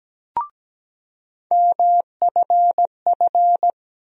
Morse code, heard as a keyed tone, about 17 wpm, 700 Hz. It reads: MFF
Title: USA Call Sign Postfixes - 3 Letters 17wpm